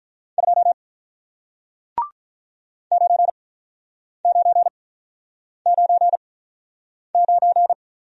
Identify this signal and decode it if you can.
Morse code, heard as a keyed tone, about 45 wpm, spaced as for 9 wpm, 700 Hz. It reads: 2 9999